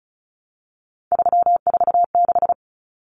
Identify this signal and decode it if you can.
Morse code, heard as a keyed tone, about 34 wpm, 700 Hz. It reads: 346
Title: Common Cut Numbers - 3-Digit 35wpm